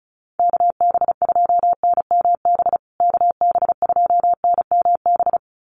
Morse code, heard as a keyed tone, about 34 wpm, 700 Hz. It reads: KB2NMB KB2NMB
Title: USA Call Signs Repeated 2x 35wpm